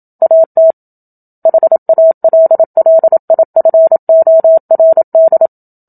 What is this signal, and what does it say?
Morse code, heard as a keyed tone, about 27 wpm, 650 Hz. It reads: AT HALLIFORD
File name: Book Two / 01-Chapter 1 - 27 WPM